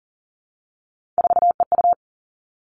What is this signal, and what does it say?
Morse code, heard as a keyed tone, about 38 wpm, 700 Hz. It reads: 4EU
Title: Three Challenging Letter-Number Combinations 40wpm